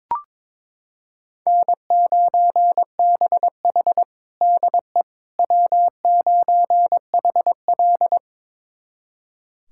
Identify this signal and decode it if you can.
Morse code, heard as a keyed tone, about 22 wpm, 700 Hz. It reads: N9BH DE W9HL
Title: USA Callsign DE Callsign - 4 Characters - 22wpm